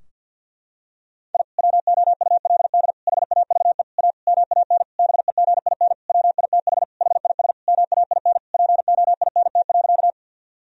Morse code, heard as a keyed tone, about 52 wpm, 700 Hz. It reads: I WOULD HAVE A MAN BEGIN WITH HIS MAIN POINT.
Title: Sentences from Top 1000 Words - Review Part 2 - 50wpm